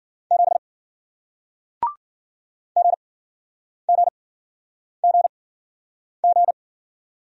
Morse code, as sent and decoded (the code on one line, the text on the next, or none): --... / --. --. --. --.
7 GGGG